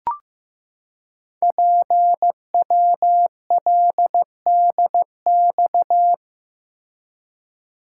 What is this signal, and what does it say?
Morse code, heard as a keyed tone, about 15 wpm, 700 Hz. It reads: PWLDX
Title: Code Groups - 5 Letter:Numbers - 15wpm